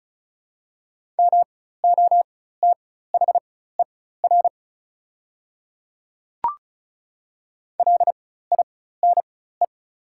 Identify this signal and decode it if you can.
Morse code, heard as a keyed tone, about 35 wpm, spaced as for 16 wpm, 690 Hz. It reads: MOTHER LINE
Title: 4x Character Spacing - Top 200 Words - 35wpm